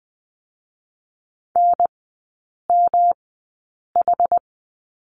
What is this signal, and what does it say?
Morse code, heard as a keyed tone, about 19 wpm, 700 Hz. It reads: N M H
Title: 107 Lesson 18e - Letter m - Three Characters 20wpm